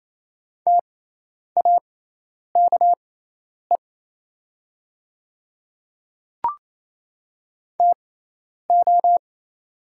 Morse code, heard as a keyed tone, about 28 wpm, spaced as for 10 wpm, 700 Hz. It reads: TAKE TO
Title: 6x Character Spacing - Top 200 Words - 28wpm